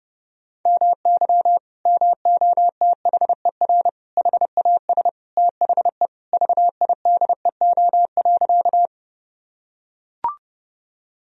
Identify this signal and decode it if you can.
Morse code, heard as a keyed tone, about 32 wpm, 700 Hz. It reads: MY MOTHER HAS THE VIDEO.
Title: Sentences from Top 1000 Words - Review Part 4 - 30wpm